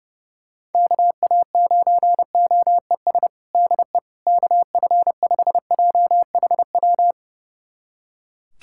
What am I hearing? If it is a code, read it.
Morse code, heard as a keyed tone, about 32 wpm, 700 Hz. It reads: KA9OES DE KF5JHW
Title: USA Callsign DE Callsign 30wpm